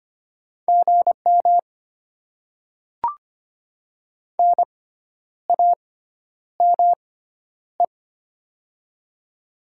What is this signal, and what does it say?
Morse code, heard as a keyed tone, about 25 wpm, spaced as for 10 wpm, 700 Hz. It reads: GM NAME
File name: Top 20 QSO Elements - 6x Character Spacing - 25wpm